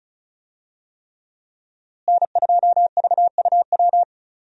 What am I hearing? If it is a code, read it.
Morse code, heard as a keyed tone, about 36 wpm, 700 Hz. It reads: N2VUW
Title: USA Call Signs - 5 Characters - Repeated 3x - 35wpm